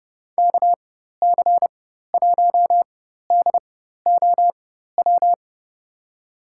Morse code, heard as a keyed tone, about 30 wpm, spaced as for 14 wpm, 700 Hz. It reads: KC1DOW
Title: USA Call Signs - 6 Characters - 4x Character Spacing - 30wpm